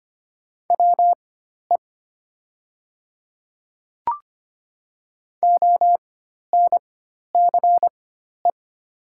Morse code, heard as a keyed tone, about 25 wpm, spaced as for 12 wpm, 700 Hz. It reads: WE ONCE
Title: 4x Character Spacing - Top 300 Words - 25wpm